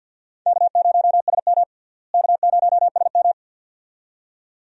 Morse code, heard as a keyed tone, about 52 wpm, 700 Hz. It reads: K0SM K0SM
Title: USA Call Signs - 4 Characters - Repeated 2x - 50wpm